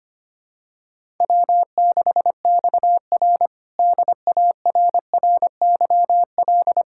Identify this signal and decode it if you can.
Morse code, heard as a keyed tone, about 25 wpm, 700 Hz. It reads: W6XR DARRYL
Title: K1USN SST QSO Practice - 25wpm